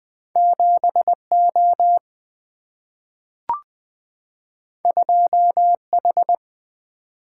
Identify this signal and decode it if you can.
Morse code, heard as a keyed tone, about 20 wpm, 700 Hz. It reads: 7O 2H
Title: Single-Letter Single-Number 20wpm